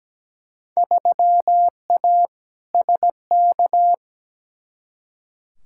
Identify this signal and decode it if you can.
Morse code, heard as a keyed tone, about 17 wpm, 700 Hz. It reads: 3A SK
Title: Field Day Exchanges Encore - 17wpm